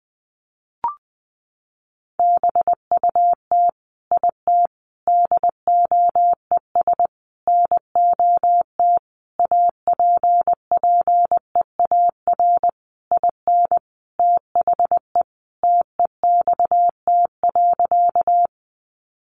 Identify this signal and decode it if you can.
Morse code, heard as a keyed tone, about 19 wpm, 700 Hz. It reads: BUT IT DOES NOT APPEAR IN THE TEXT.
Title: Sentences from Top 1200 Words 20wpm